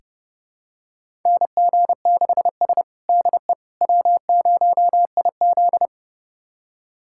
Morse code, heard as a keyed tone, about 30 wpm, 700 Hz. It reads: NG6S DE W0IZ
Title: USA Callsign DE Callsign - 4 Characters - 30wpm